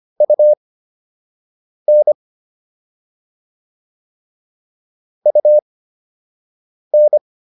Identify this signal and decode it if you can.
Morse code, heard as a keyed tone, about 25 wpm, spaced as for 6 wpm, 600 Hz. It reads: UN UN
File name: UN5 06